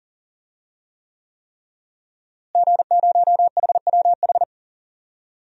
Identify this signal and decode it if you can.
Morse code, heard as a keyed tone, about 37 wpm, 700 Hz. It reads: G0HWH
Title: DXCC-100 Call Signs - 5 Characters - Repeated 3x - 40wpm